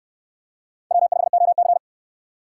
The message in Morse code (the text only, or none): W5OC